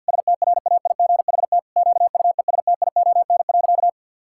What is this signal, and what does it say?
Morse code, heard as a keyed tone, about 50 wpm, 700 Hz. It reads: STRAIGHT QUESTION.
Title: Sentences from Top 1000 Words - Review Part 6 - 50wpm